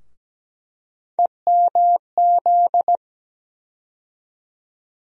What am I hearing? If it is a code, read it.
Morse code, heard as a keyed tone, about 17 wpm, 700 Hz. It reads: EMZ
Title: USA Call Sign Postfixes - 3 Letters 17wpm